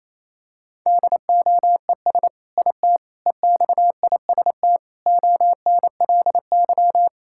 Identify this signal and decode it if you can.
Morse code, heard as a keyed tone, about 28 wpm, 700 Hz. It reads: DOES IT EXIST ONLY
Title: Sentences from Top 1000 Words - Review Part 5 - 28wpm